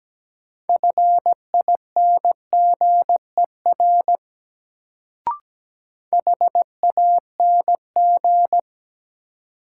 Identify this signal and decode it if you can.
Morse code, heard as a keyed tone, about 17 wpm, 700 Hz. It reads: FINGER HANG
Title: Sentences from Top 1600 Words - New Word Drill Alt - 17wpm